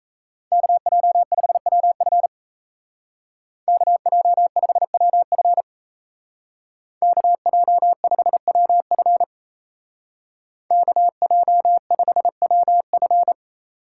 Morse code, heard as a keyed tone, about 33 wpm, 700 Hz. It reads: KJ5WF KJ5WF KJ5WF KJ5WF
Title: Speed Racing - USA 2x2 Call Signs 28wpm